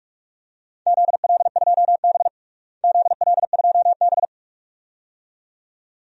Morse code, heard as a keyed tone, about 45 wpm, 700 Hz. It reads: ZL2B ZL2B
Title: DXCC-100 Call Signs - 4 Characters - Repeated 2x - 45wpm